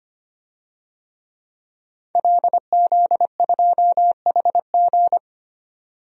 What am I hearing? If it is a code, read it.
Morse code, heard as a keyed tone, about 25 wpm, 700 Hz. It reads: LZ2HG
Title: DXCC-100 Call Signs - 5 Characters - 25wpm